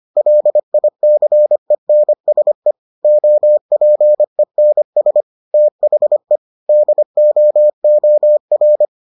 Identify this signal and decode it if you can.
Morse code, heard as a keyed tone, about 25 wpm, 600 Hz. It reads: LICENSE OPENS THE DOOR